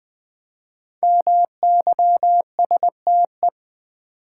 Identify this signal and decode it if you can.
Morse code, heard as a keyed tone, about 20 wpm, 700 Hz. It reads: MYSTE
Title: Build A Word - 5 Letters At A Time - 10-Letter Words - 20wpm